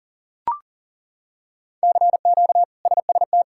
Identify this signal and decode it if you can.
Morse code, heard as a keyed tone, about 40 wpm, 700 Hz. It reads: CQ SST
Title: K1USN SST QSO Practice - 40wpm